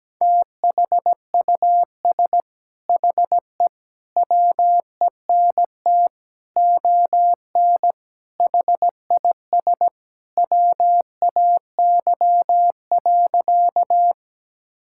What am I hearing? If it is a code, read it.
Morse code, heard as a keyed tone, about 17 wpm, 700 Hz. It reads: THUS HE WENT ON HIS WAY.
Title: Sentences from Top 1000 Words - Review Part 5 - 17wpm